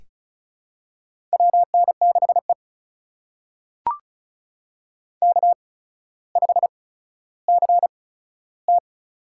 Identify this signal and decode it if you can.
Morse code, heard as a keyed tone, about 36 wpm, spaced as for 15 wpm, 700 Hz. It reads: WN6E K5CT
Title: USA Call Signs - 4 Characters - 8x Character Spacing - 35wpm